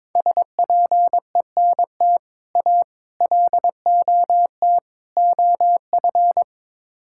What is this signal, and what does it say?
Morse code, heard as a keyed tone, about 22 wpm, 700 Hz. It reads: SPENT A LOT OF
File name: practice / 22wpm / Build A Phrase - 5-Word Phrases - 22wpm